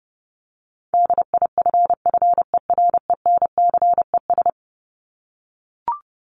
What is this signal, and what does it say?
Morse code, heard as a keyed tone, about 30 wpm, 700 Hz. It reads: DIFFERENCES
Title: ES:S Suffix - Top 100 Words with Base Words 30wpm